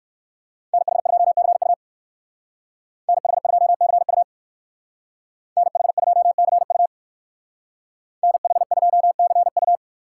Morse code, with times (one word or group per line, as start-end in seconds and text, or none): 0.7-1.7 NH2CU
3.1-4.2 NH2CU
5.6-6.9 NH2CU
8.2-9.8 NH2CU